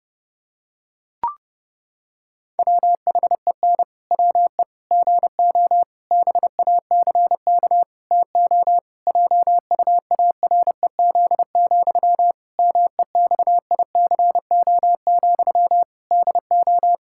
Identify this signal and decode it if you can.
Morse code, heard as a keyed tone, about 30 wpm, 700 Hz. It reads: WHEN WE GO BACK TO JUAREZ, MEXICO, DO
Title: Pangram Challenge v2 - 30wpm